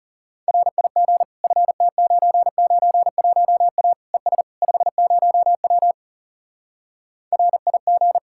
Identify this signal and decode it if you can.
Morse code, heard as a keyed tone, about 40 wpm, 700 Hz. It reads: RIG FT991A ES 50W RIG